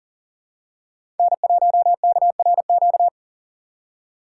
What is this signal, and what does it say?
Morse code, heard as a keyed tone, about 39 wpm, 690 Hz. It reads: N1KRQ
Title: USA Call Signs - 4x Character Spacing - 40wpm